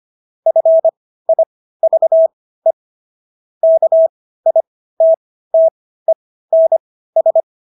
Morse code, heard as a keyed tone, about 25 wpm, spaced as for 15 wpm, 650 Hz. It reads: FIVE KITTENS